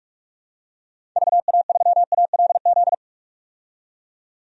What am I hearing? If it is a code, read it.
Morse code, heard as a keyed tone, about 43 wpm, 700 Hz. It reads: VA3ALZ
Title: DXCC-100 Call Signs - 6 Characters - 45wpm